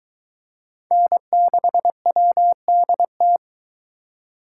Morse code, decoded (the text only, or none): N6WDT